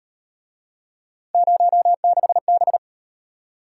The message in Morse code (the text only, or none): 06B